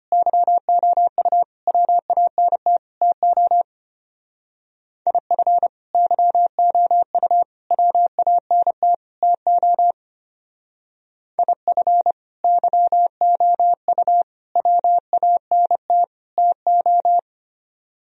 Morse code, heard as a keyed tone, about 31 wpm, 700 Hz. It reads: YOU WANT TO IF YOU WANT TO IF YOU WANT TO
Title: Speed-Racing - Sets of 4 Words 25wpm